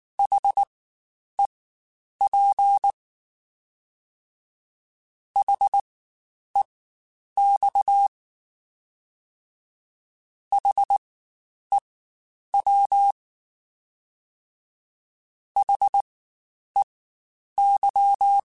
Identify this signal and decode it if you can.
Morse code, heard as a keyed tone, about 19 wpm, spaced as for 9 wpm, 800 Hz. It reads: HEP HEX HEW HEY
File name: Spaced Alphas / Spaced Alphas mp3s / Hs sp @18